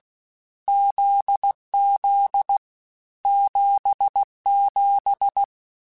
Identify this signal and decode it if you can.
Morse code, heard as a keyed tone, about 16 wpm, 800 Hz. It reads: ZZ 77
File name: Short Runs / Short Runs-mp3s / GZ GZ7 @15